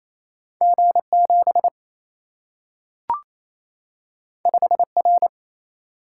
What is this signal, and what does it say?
Morse code, heard as a keyed tone, about 28 wpm, 700 Hz. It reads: G 7 5 R
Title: Single-Letter Single-Number 28wpm